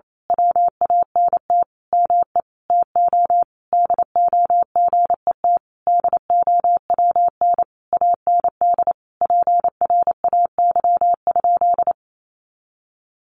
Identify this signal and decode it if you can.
Morse code, heard as a keyed tone, about 28 wpm, 700 Hz. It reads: WANT ME TO DO GET DOWN AND PRAY?
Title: A Sound Of Thunder 28wpm